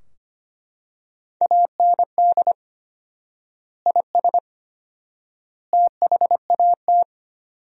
Morse code, heard as a keyed tone, about 25 wpm, 700 Hz. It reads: AND IS THAT
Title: 4x Word-Spacing - Sentences from Top 100 Words - 25wpm